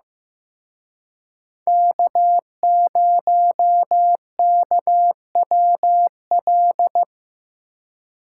Morse code, heard as a keyed tone, about 18 wpm, 700 Hz. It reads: K0KWL
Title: Speed Racing - USA 1x3 Call Signs 15wpm